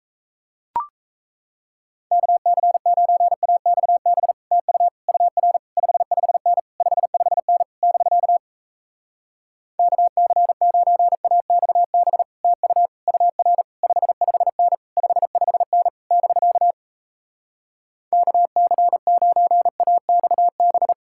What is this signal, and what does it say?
Morse code, heard as a keyed tone, about 37 wpm, 700 Hz. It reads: KC9AXB TU UR 55N 55N <BK> KC9AXB TU UR 55N 55N <BK> KC9AXB